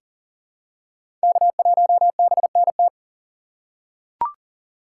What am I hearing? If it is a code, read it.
Morse code, heard as a keyed tone, about 37 wpm, 700 Hz. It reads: K1BNT
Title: USA Call Signs - 5 Characters - 2x Character Spacing - 40wpm